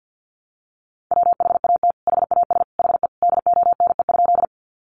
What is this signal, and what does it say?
Morse code, heard as a keyed tone, about 50 wpm, 700 Hz. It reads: WHAT HAS HE DONE?
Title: Sentences from Top 500 Words 50wpm